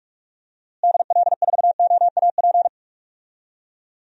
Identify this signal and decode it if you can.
Morse code, heard as a keyed tone, about 46 wpm, 700 Hz. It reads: DL4OAP
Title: USA Call Signs - 6 Characters - Repeated 3x - 45wpm